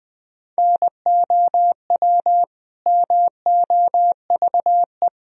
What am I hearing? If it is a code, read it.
Morse code, heard as a keyed tone, about 19 wpm, 700 Hz. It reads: NOW MOVE
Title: Sentences from Top 1000 Words - Review Part 1 - 20wpm